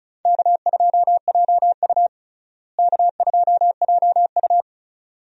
Morse code, heard as a keyed tone, about 34 wpm, 700 Hz. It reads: K2JU K2JU
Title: USA Call Signs - 4 Characters - Repeated 2x - 35wpm